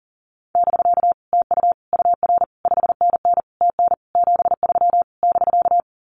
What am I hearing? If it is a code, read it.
Morse code, heard as a keyed tone, about 40 wpm, 700 Hz. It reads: <BK> TU UR 5NN TN 73 <BK>
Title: Common QSO Phrases Encore 40wpm